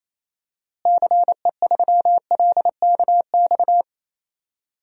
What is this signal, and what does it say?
Morse code, heard as a keyed tone, about 28 wpm, 700 Hz. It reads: CE3LKX
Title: DXCC-100 Call Signs - 6 Characters - Repeated 2x - 28wpm